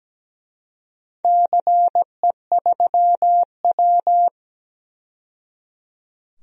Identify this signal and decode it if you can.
Morse code, heard as a keyed tone, about 17 wpm, 700 Hz. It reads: CE3W
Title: DXCC-100 Call Signs - 4 Characters - 17wpm